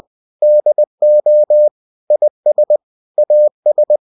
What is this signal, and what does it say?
Morse code, heard as a keyed tone, about 20 wpm, 600 Hz. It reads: DO IS AS
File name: cw-books / winnie-the-pooh / winnie-20wpm-page-0030